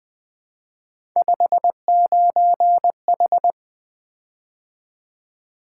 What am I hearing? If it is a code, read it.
Morse code, heard as a keyed tone, about 20 wpm, 700 Hz. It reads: 59H
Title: Three Letter-Number Combinations - 20wpm